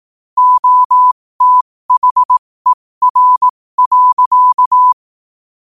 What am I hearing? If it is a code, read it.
Morse code, heard as a keyed tone, about 18 wpm, spaced as for 15 wpm, 1000 Hz. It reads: OTHER.
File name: qotdfiles / 15wpm